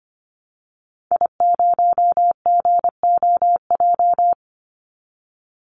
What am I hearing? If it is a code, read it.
Morse code, heard as a keyed tone, about 25 wpm, 700 Hz. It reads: I0GOJ
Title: Call Signs - 25wpm